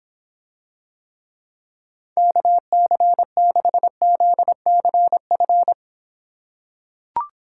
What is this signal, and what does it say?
Morse code, heard as a keyed tone, about 24 wpm, 700 Hz. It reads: KC6ZCF